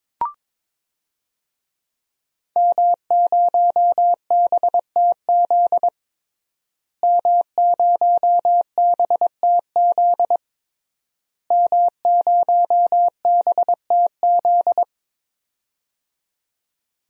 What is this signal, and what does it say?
Morse code, heard as a keyed tone, about 22 wpm, 700 Hz. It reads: M0BTZ M0BTZ M0BTZ
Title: DXCC-100 Call Signs - 5 Characters - Repeated 3x - 22wpm